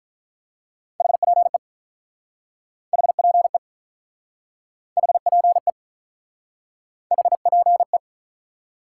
Morse code, heard as a keyed tone, about 48 wpm, 700 Hz. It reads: HPE HPE HPE HPE